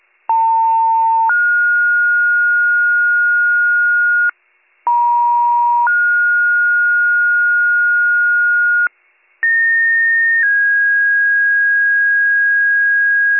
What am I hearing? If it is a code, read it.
Two-tone page, heard as a one-second tone, then a three-second tone, about 900 then 1430 Hz, about 950 then 1430 Hz, about 1850 then 1650 Hz, with noise. It is heard from 0.3 to 13.4 s.